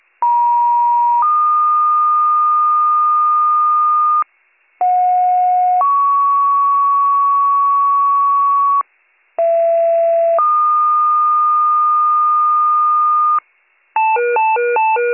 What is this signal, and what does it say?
Two-tone page, heard as a one-second tone, then a three-second tone, about 950 then 1220 Hz, about 730 then 1120 Hz, about 670 then 1220 Hz. It is heard from 0.2 to 13.4 s.